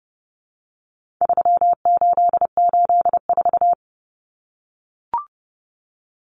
Morse code, heard as a keyed tone, about 30 wpm, 700 Hz. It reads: 3884
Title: Common Cut Numbers - 4-Digit 30wpm